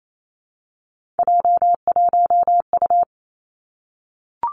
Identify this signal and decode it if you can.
Morse code, heard as a keyed tone, about 28 wpm, 700 Hz. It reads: J1U
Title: Three Challenging Letter-Number Combinations 28wpm